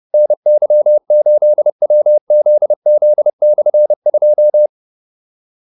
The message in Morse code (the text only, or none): NY8WZZ/2